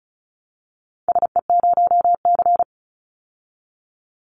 Morse code, heard as a keyed tone, about 34 wpm, 700 Hz. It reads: SE0C
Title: Call Signs Repeated 3x - 35wpm